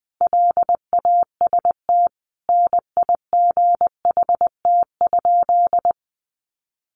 Morse code, heard as a keyed tone, about 20 wpm, 700 Hz. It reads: LAST NIGHT?
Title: Sentences from Top 300 Words 20wpm